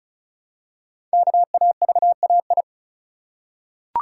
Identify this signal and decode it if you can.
Morse code, heard as a keyed tone, about 36 wpm, 700 Hz. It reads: KAVAI
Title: Code Groups - 5 Letters - 35wpm